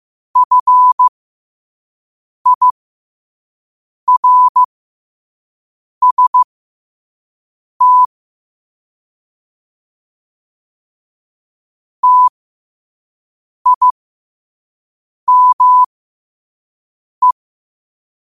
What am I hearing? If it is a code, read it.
Morse code, heard as a keyed tone, about 15 wpm, spaced as for 5 wpm, 1000 Hz. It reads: FIRST TIME
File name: qotdfiles / 5wpm